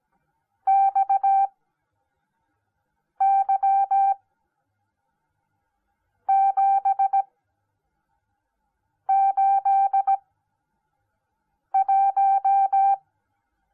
Morse code, heard as a keyed tone, about 17 wpm, spaced as for 4 wpm, 780 Hz. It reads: XY781